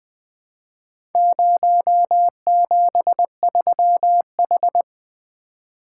Morse code, heard as a keyed tone, about 20 wpm, 700 Hz. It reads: 073H
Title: Four Letter-Number Combinations - 20wpm